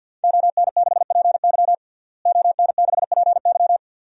Morse code, heard as a keyed tone, about 51 wpm, 700 Hz. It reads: ON6PY ON6PY
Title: DXCC-100 Call Signs - 5 Characters - Repeated 2x - 50wpm